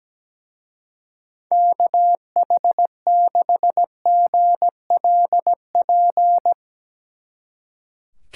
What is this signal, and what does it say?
Morse code, heard as a keyed tone, about 20 wpm, 700 Hz. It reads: KH6GLP
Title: Speed Racing - USA 2x3 Call Signs 17wpm